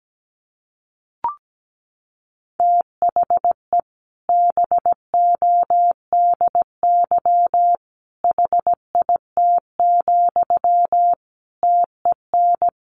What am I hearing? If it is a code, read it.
Morse code, heard as a keyed tone, about 17 wpm, 700 Hz. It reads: THE BODY HIT, TEN
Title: A Sound Of Thunder 17wpm